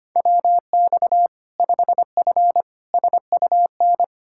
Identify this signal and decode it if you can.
Morse code, heard as a keyed tone, about 25 wpm, 700 Hz. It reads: WX 5F SUN
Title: Practice QSO - Exchange 2 - Info - 25wpm